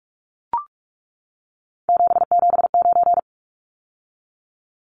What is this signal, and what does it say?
Morse code, heard as a keyed tone, about 44 wpm, 700 Hz. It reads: 779
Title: Numbers - 3-Digit 45wpm